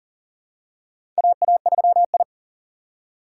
Morse code, heard as a keyed tone, about 40 wpm, 700 Hz. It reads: AA3I
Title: USA Call Signs - 4 Characters - 4x Character Spacing - 40wpm